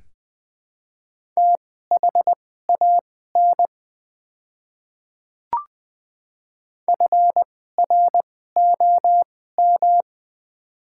Morse code, heard as a keyed tone, about 20 wpm, spaced as for 15 wpm, 690 Hz. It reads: THAN FROM
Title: 2x Character Spacing - Top 300 Words - 20wpm